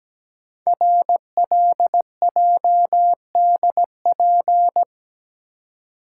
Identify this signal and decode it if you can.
Morse code, heard as a keyed tone, about 17 wpm, 700 Hz. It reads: RLJDP
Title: Code Groups - 5 Letters - 17wpm